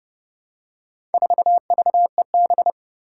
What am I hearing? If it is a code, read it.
Morse code, heard as a keyed tone, about 28 wpm, 700 Hz. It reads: 4VEB
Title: Four Challenging Letter-Number Combinations 30wpm